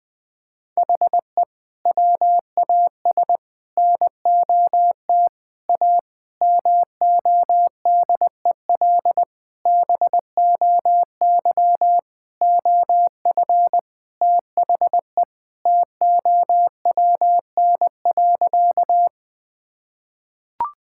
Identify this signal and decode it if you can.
Morse code, heard as a keyed tone, about 20 wpm, 700 Hz. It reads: HE WAS NOT A MODEL BOY OF THE TOWN.
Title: Sentences from Top 1000 Words - Review Part 5 - 20wpm